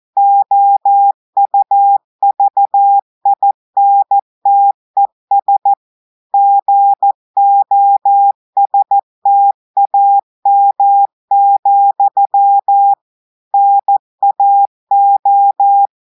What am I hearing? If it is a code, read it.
Morse code, heard as a keyed tone, about 14 wpm, 800 Hz. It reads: OUVINTES GOSTAM, NAO
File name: audios / portugais / portugais texte en clair 2 vitesse 14 mots minutes